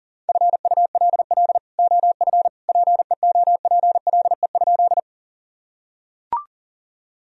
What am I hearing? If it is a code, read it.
Morse code, heard as a keyed tone, about 37 wpm, 700 Hz. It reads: FULL OF PEOPLE?
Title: Sentences from Top 1000 Words - Review Part 3 - 40wpm